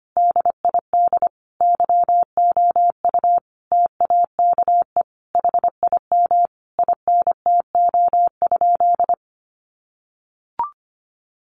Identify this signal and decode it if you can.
Morse code, heard as a keyed tone, about 25 wpm, 700 Hz. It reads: DID YOU TAKE HIM INTO?
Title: Sentences from Top 600 Words 25wpm